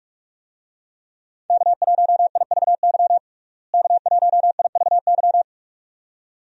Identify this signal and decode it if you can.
Morse code, heard as a keyed tone, about 46 wpm, 700 Hz. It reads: K1IVY K1IVY
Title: USA Call Signs - 5 Characters - Repeated 2x - 45wpm